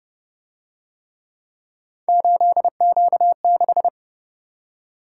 Morse code, heard as a keyed tone, about 28 wpm, 700 Hz. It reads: 8Q6